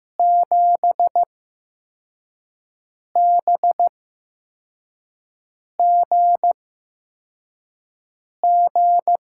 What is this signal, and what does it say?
Morse code, heard as a keyed tone, about 15 wpm, spaced as for 4 wpm, 700 Hz. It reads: 7BGG
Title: USA Call Signs - 8x Character Spacing - 15wpm